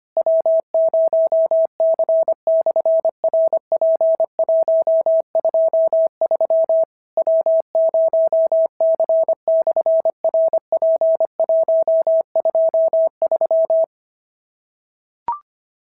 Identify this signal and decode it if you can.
Morse code, heard as a keyed tone, about 25 wpm, 650 Hz. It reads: W0C/RP123 W0C/RP123
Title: SOTA QSO Practice 25wpm